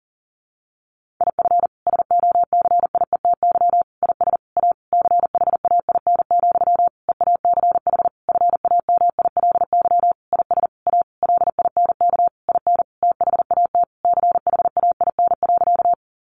Morse code, heard as a keyed tone, about 40 wpm, 700 Hz. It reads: IF SOCIETY IS A CHAIN, EACH FAMILY IS A LINK IN THAT CHAIN.
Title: Sentences from Top 1500 Words 40wpm